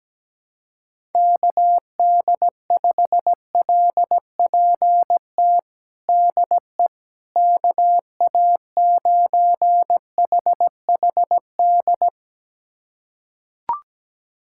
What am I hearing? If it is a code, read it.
Morse code, heard as a keyed tone, about 17 wpm, 700 Hz. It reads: KD5LPT DE KA9HHD
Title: USA Callsign DE Callsign 17wpm